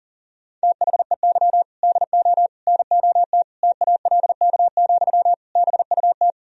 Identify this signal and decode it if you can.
Morse code, heard as a keyed tone, about 40 wpm, 700 Hz. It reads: THEY DO NOT TALK, BUT
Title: Sentences from Top 1000 Words - Review Part 1 - 40wpm